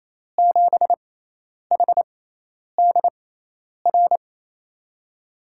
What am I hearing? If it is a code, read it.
Morse code, heard as a keyed tone, about 28 wpm, spaced as for 10 wpm, 700 Hz. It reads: 7HDR
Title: USA Call Signs - 6 Characters - 6x Character Spacing - 28wpm